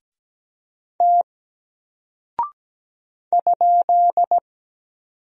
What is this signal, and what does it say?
Morse code, heard as a keyed tone, about 17 wpm, 700 Hz. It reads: T ?
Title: Single Character (Letter:Number:Punctuation:Prosign) Alt 17wpm